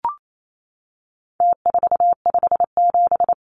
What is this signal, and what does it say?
Morse code, heard as a keyed tone, about 28 wpm, 700 Hz. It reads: T457
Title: Numbers - Contest Style - 4-Digit 28wpm